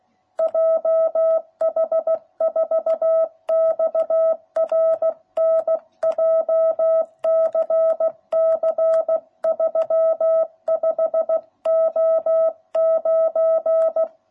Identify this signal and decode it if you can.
Morse code, heard as a keyed tone, about 16 wpm, 660 Hz. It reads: JH4XRN JCC 35O9